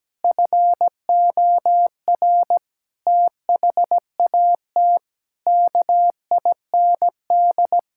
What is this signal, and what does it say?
Morse code, heard as a keyed tone, about 17 wpm, 700 Hz. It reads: FOR THAT KIND
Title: Sentences from Top 1000 Words - Review Part 6 - 17wpm